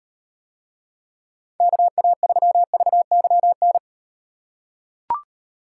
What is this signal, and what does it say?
Morse code, heard as a keyed tone, about 35 wpm, 700 Hz. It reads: KA3VYN